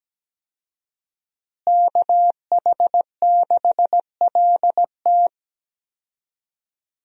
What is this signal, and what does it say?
Morse code, heard as a keyed tone, about 17 wpm, 700 Hz. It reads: KH6LT
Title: USA Call Signs - 5 Characters - Repeated 2x - 17wpm